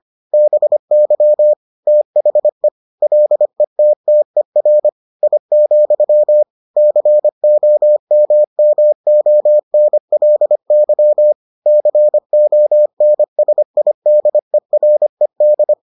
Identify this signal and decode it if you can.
Morse code, heard as a keyed tone, about 25 wpm, 600 Hz. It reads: BY THE LETTER I, COMMONLY CONSIDERED